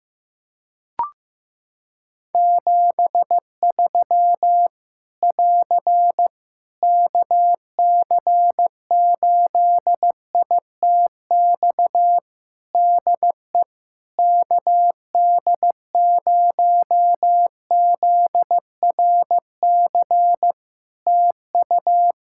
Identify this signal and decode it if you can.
Morse code, heard as a keyed tone, about 15 wpm, 700 Hz. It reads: 73 <AR> KC8ITX DE KD0ZRC TU
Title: Practice QSO - Exchange 4 - Ending - 15wpm